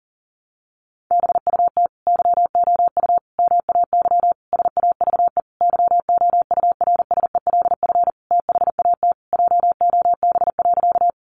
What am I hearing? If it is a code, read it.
Morse code, heard as a keyed tone, about 40 wpm, 700 Hz. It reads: BUT YOU MAY SAVE YOURSELF THAT JOB.
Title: Sentences from Top 1100 Words 40wpm